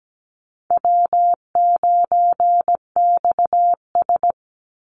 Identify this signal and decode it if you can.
Morse code, heard as a keyed tone, about 17 wpm, 700 Hz. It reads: W9XS
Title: Easy Call Signs - 17wpm